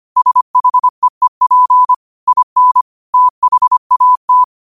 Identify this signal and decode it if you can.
Morse code, heard as a keyed tone, about 25 wpm, 1000 Hz. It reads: SHEEP IN THAT
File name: qotdfiles / 25wpm